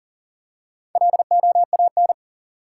Morse code, heard as a keyed tone, about 40 wpm, 700 Hz. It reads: LOAN
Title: Four Letter Words - 40wpm